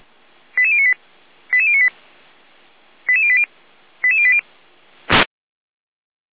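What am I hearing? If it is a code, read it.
Five-tone selective call (ZVEI), heard as quick runs of five tones, tones of about 80 ms. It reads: 88098 88098 88080 88080
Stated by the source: ZVEI five-tone selective call, received on an SDR.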